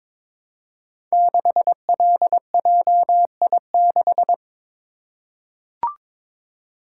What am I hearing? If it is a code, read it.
Morse code, heard as a keyed tone, about 22 wpm, 700 Hz. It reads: 6LJI6
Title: Code Groups - 5 Letter:Numbers - 22wpm